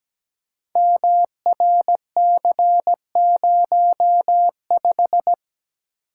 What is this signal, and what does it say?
Morse code, heard as a keyed tone, about 17 wpm, 700 Hz. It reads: MRC05
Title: Code Groups - 5 Letter:Numbers - 17wpm